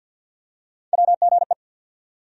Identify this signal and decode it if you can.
Morse code, heard as a keyed tone, about 47 wpm, 690 Hz. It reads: WGE
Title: USA Call Sign Postfixes - 3 Letters 50wpm